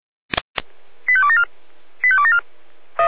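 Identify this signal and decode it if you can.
Five-tone selective call (ZVEI), heard as quick runs of five tones, tones of about 70 ms. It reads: 85274 85274
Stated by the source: ZVEI five-tone selective call 85211 received by SDR.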